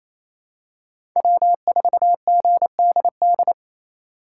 Morse code, heard as a keyed tone, about 28 wpm, 700 Hz. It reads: W4GDD
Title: USA Call Signs - 5 Characters - 4x Character Spacing - 28wpm